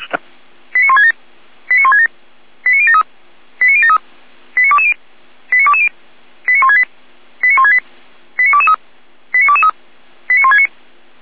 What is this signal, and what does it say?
Five-tone selective call (ZVEI), heard as quick runs of five tones, tones of about 70 ms. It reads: 89167 89167 89073 89073 89220 89220 89168 89168 89223 89223 89169